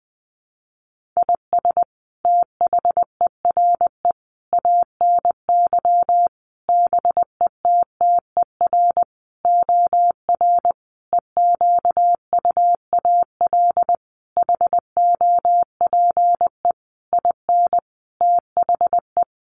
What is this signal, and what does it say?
Morse code, heard as a keyed tone, about 20 wpm, 700 Hz. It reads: IS THERE ANY BETTER OR EQUAL HOPE IN THE
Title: Sentences from Top 1400 Words 20wpm